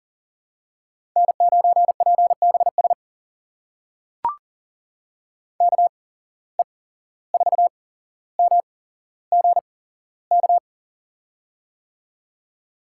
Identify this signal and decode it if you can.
Morse code, heard as a keyed tone, about 40 wpm, spaced as for 16 wpm, 700 Hz. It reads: N9PBS KE4MGK
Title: USA Call Signs - 8x Character Spacing - 40wpm